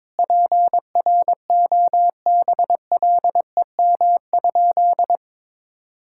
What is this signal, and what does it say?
Morse code, heard as a keyed tone, about 22 wpm, 700 Hz. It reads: PROBLEM?
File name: practice / 22wpm / Sentences with 8-Word Phrases - 22wpm